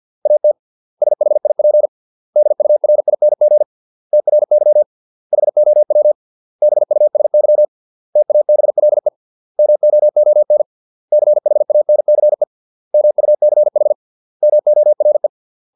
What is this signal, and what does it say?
Morse code, heard as a keyed tone, about 50 wpm, 600 Hz. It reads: AT SHIP DURING TRY HOW BUSY TABLE MOON CHANCE MUCH MORE